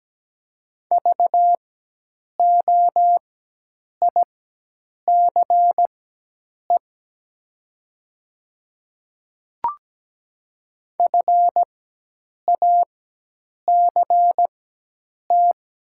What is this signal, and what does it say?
Morse code, heard as a keyed tone, about 17 wpm, spaced as for 8 wpm, 700 Hz. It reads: VOICE FACT
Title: 4x Character Spacing - Top 500 Words - 17wpm